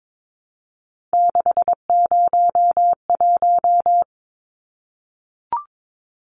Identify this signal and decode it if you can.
Morse code, heard as a keyed tone, about 22 wpm, 700 Hz. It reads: 601
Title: Numbers - 3-Digit 22wpm